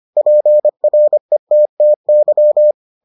Morse code, heard as a keyed tone, about 25 wpm, 600 Hz. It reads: PRETTY